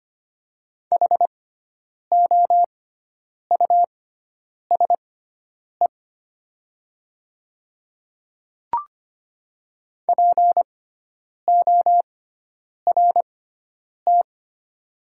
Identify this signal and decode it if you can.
Morse code, heard as a keyed tone, about 25 wpm, spaced as for 9 wpm, 700 Hz. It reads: HOUSE PORT
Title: 6x Character Spacing - Top 300 Words - 25wpm